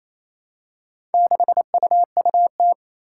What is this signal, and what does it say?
Morse code, heard as a keyed tone, about 28 wpm, 700 Hz. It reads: 6UUT
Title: Four Challenging Letter-Number Combinations 28wpm